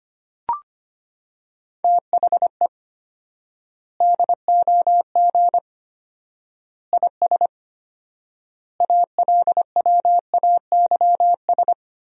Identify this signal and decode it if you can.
Morse code, heard as a keyed tone, about 25 wpm, 700 Hz. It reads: THE DOG IS ALWAYS
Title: 4x Word-Spacing - Sentences from Top 400 Words - 25wpm